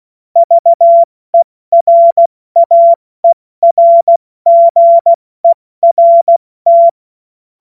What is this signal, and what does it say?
Morse code, heard as a keyed tone, about 16 wpm, 680 Hz. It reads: VERAERGERT
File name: Morsekurs / Witze / Witz 21. März 2023 70 F80